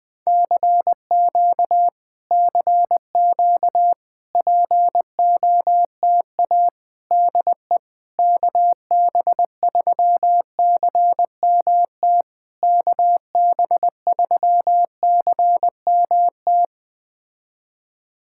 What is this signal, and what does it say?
Morse code, heard as a keyed tone, about 20 wpm, 700 Hz. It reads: CQ CQ POTA DE KB3CMT KB3CMT
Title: POTA QSO Practice - 20wpm